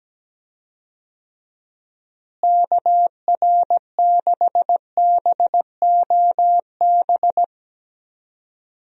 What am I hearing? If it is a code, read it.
Morse code, heard as a keyed tone, about 17 wpm, 700 Hz. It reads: KR6BOB
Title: USA Call Signs - 6 Characters - Repeated 3x - 17wpm